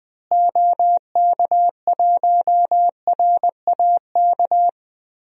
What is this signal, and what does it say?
Morse code, heard as a keyed tone, about 21 wpm, 700 Hz. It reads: OK1RAK
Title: DXCC-100 Call Signs - 6 Characters - Repeated 3x - 20wpm